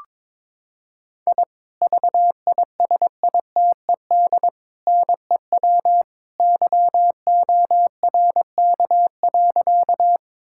Morse code, heard as a keyed tone, about 21 wpm, 700 Hz. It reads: I VISITED NEW YORK.
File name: practice / 22wpm / Sentences with 2-Word Phrases - 22wpm